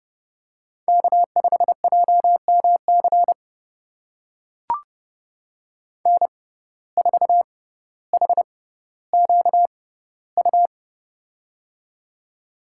Morse code, heard as a keyed tone, about 30 wpm, spaced as for 15 wpm, 700 Hz. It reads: K5JMC N4HQU
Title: USA Call Signs - 5 Characters - 6x Character Spacing - 30wpm